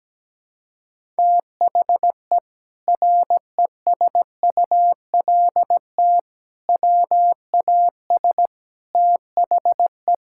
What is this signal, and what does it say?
Morse code, heard as a keyed tone, about 17 wpm, 700 Hz. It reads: THE RESULT WAS THE
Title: Sentences from Top 1000 Words - Review Part 5 - 17wpm